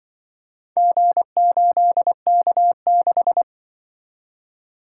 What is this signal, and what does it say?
Morse code, heard as a keyed tone, about 25 wpm, 700 Hz. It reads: G8K6
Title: Speed-Racing - Four Letter-Number Combinations - 20wpm